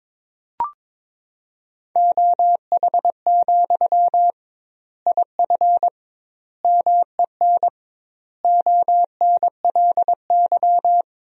Morse code, heard as a keyed tone, about 22 wpm, 700 Hz. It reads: OH, IF MEN ONLY
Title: 2x Word-Spacing - Sentences from Top 500 Words - 22wpm